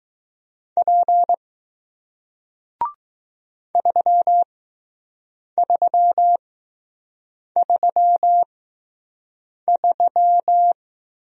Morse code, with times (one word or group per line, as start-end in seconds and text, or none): 0.8-1.3 P
3.8-10.7 3333